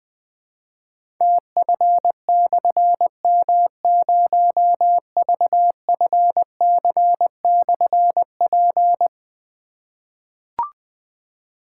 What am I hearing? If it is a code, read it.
Morse code, heard as a keyed tone, about 20 wpm, 700 Hz. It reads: TF/M0VFC/P
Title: Difficult Call Signs 20wpm